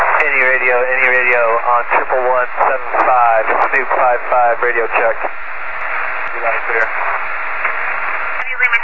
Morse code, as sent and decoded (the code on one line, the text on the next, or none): none